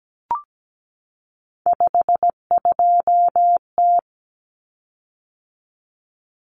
Morse code, heard as a keyed tone, about 17 wpm, 700 Hz. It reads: 52T
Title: Common Cut Numbers - 3-Digit 17wpm